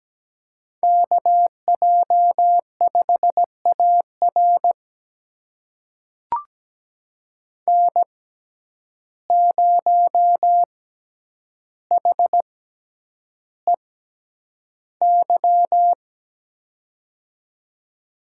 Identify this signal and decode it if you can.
Morse code, heard as a keyed tone, about 17 wpm, spaced as for 9 wpm, 700 Hz. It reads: KJ5AR N0HEY
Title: USA Call Signs - 5 Characters - 6x Character Spacing - 17wpm